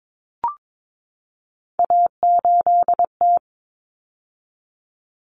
Morse code, heard as a keyed tone, about 22 wpm, 700 Hz. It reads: A8T
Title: Common Cut Numbers - 3-Digit 22wpm